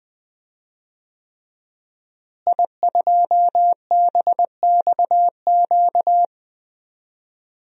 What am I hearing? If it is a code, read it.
Morse code, heard as a keyed tone, about 20 wpm, 700 Hz. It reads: I2BXQ
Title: DXCC-100 Call Signs - 5 Characters - Repeated 3x - 20wpm